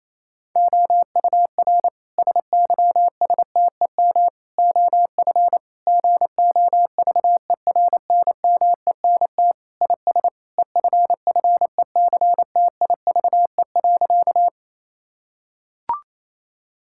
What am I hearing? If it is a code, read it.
Morse code, heard as a keyed tone, about 28 wpm, 700 Hz. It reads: OUR SYSTEM OF GOVERNMENT IS EFFECTIVE.
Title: Sentences from Top 1000 Words - Review Part 6 - 28wpm